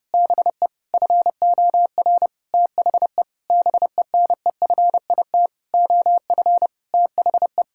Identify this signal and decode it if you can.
Morse code, heard as a keyed tone, about 30 wpm, 700 Hz. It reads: BE FOR THE BENEFIT OF THE
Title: Sentences from Top 1000 Words - Review Part 6 - 30wpm